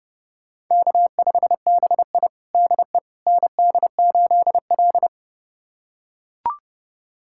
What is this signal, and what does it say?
Morse code, heard as a keyed tone, about 30 wpm, 700 Hz. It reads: K5BI DE ND8L
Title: USA Callsign DE Callsign - 4 Characters - 30wpm